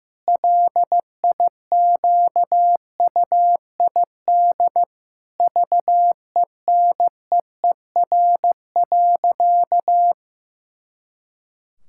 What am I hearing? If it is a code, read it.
Morse code, heard as a keyed tone, about 15 wpm, 700 Hz. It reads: LIQUID VENEER.
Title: Pangram Challenge v2 - 15wpm